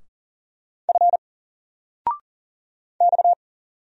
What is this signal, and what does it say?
Morse code, heard as a keyed tone, about 39 wpm, 700 Hz. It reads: F X
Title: Single Character (Letter:Number:Punctuation:Prosign) 40wpm